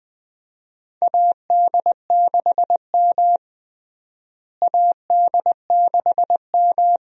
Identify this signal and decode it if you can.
Morse code, heard as a keyed tone, about 20 wpm, 700 Hz. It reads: AD6M AD6M
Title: USA Call Signs - 4 Characters - Repeated 2x - 20wpm